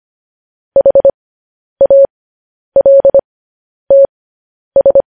Morse code, heard as a keyed tone, about 25 wpm, spaced as for 10 wpm, 550 Hz. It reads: HALTS
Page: news-headlines-in-morse-code-at-10-wpm